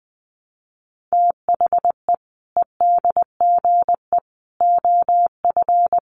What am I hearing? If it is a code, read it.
Morse code, heard as a keyed tone, about 20 wpm, 700 Hz. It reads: THE EDGE OF
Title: Sets of 3 Words 20wpm